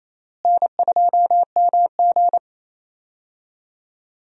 Morse code, heard as a keyed tone, about 28 wpm, 700 Hz. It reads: N2MG
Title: USA Call Signs - 4 Characters - Repeated 3x - 28wpm